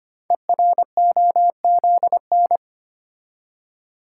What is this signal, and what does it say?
Morse code, heard as a keyed tone, about 25 wpm, 700 Hz. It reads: EROZN
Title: Code Groups - 5 Letters - 25wpm